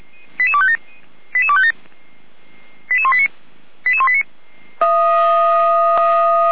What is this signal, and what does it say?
Five-tone selective call (ZVEI), heard as quick runs of five tones, tones of about 70 ms. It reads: 88257 88257 88189 88189